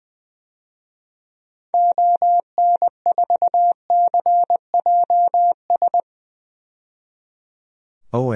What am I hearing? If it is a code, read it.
Morse code, heard as a keyed tone, about 20 wpm, 700 Hz. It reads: ON4CJS
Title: DXCC-100 Call Signs - 6 Characters - 20wpm